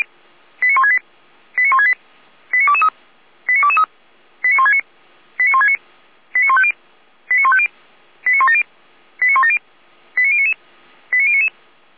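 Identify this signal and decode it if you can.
Five-tone selective call (ZVEI), heard as quick runs of five tones, tones of about 70 ms. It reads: 89168 89168 89223 89223 89169 89169 89150 89150 89170 89170 89099 89099
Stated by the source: ZVEI five-tone selective call 89167, received by SDR; fire brigade Orlamünde (feuerwehr-orlamuende).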